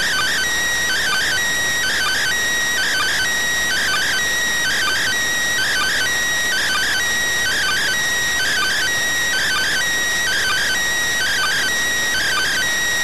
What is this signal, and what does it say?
Two-tone teleprinter signal, about 400 Hz shift.